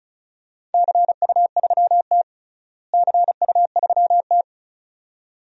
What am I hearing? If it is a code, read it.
Morse code, heard as a keyed tone, about 34 wpm, 700 Hz. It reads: CU3T CU3T
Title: DXCC-100 Call Signs - 4 Characters - Repeated 2x - 35wpm